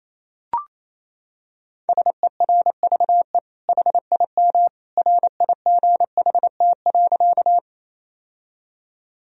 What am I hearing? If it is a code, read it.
Morse code, heard as a keyed tone, about 28 wpm, 690 Hz. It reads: SERVE HIM RIGHT.
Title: Sentences from Top 1000 Words - Review Part 2 - 28wpm